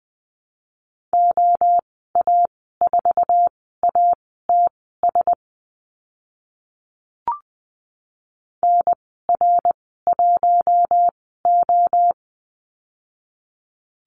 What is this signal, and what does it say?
Morse code, heard as a keyed tone, about 20 wpm, spaced as for 14 wpm, 700 Hz. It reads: OA4ATS NR1O
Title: 082 Lesson 14g - Number 2 - Call Signs 20wpm